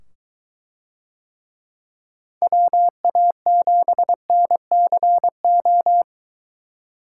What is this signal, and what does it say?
Morse code, heard as a keyed tone, about 22 wpm, 700 Hz. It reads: WA7NCO